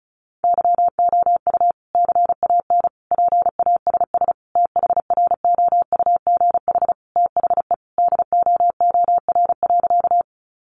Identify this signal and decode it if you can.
Morse code, heard as a keyed tone, about 34 wpm, 700 Hz. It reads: YOU CAN PASS THROUGH THE DOOR.
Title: Sentences from Top 400 Words 35wpm